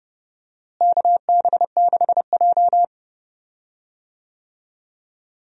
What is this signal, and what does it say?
Morse code, heard as a keyed tone, about 30 wpm, 700 Hz. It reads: KB6J
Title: USA Call Signs - 4 Characters - Repeated 2x - 30wpm